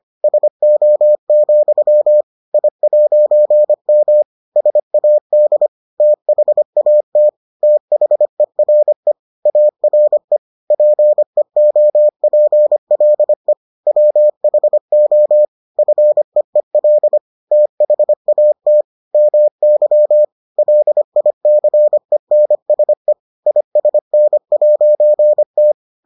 Morse code, heard as a keyed tone, about 25 wpm, 600 Hz. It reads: SO, I'M SAD THAT THERE ARE PEOPLE WHO FEEL THAT MY LICENSE ISN'T